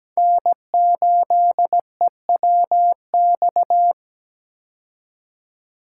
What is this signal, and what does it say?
Morse code, heard as a keyed tone, about 17 wpm, 700 Hz. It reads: N8EWX
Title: K1USN SST QSO Practice - 17wpm